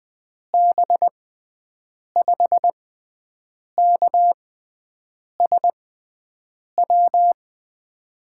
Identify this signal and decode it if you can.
Morse code, heard as a keyed tone, about 20 wpm, spaced as for 7 wpm, 700 Hz. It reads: B5KSW
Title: USA Call Signs - 6 Characters - 6x Character Spacing - 20wpm